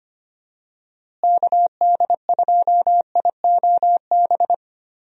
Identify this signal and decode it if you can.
Morse code, heard as a keyed tone, about 25 wpm, 700 Hz. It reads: KD2IOB
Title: USA Call Signs - 6 Characters - Repeated 3x - 25wpm